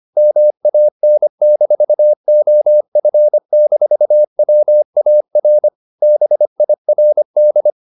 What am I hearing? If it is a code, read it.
Morse code, heard as a keyed tone, about 25 wpm, 600 Hz. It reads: MAN-OF-WAR BIRD
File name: the old man and the sea part 1 25wpm